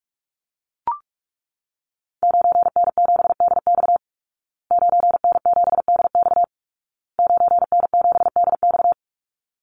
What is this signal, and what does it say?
Morse code, heard as a keyed tone, about 44 wpm, 700 Hz. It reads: 9N7DX 9N7DX 9N7DX
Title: Call Signs Repeated 2x - 45wpm